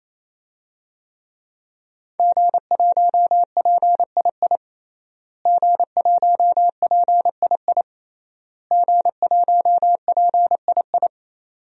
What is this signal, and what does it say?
Morse code, heard as a keyed tone, about 27 wpm, 700 Hz. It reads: G1PII G1PII G1PII
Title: DXCC-100 Call Signs - 5 Characters - Repeated 3x - 28wpm